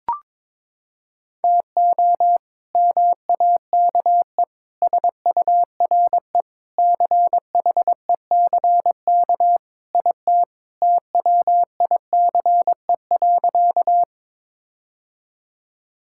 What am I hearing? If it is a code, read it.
Morse code, heard as a keyed tone, about 21 wpm, 700 Hz. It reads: TO MAKE SURE CHECK IT TWICE.
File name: practice / 22wpm / Sentences with 3-Word Phrases - 22wpm